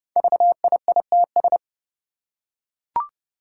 Morse code, heard as a keyed tone, about 30 wpm, 700 Hz. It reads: VIITS
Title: Five Challenging Letter-Number Combinations 30wpm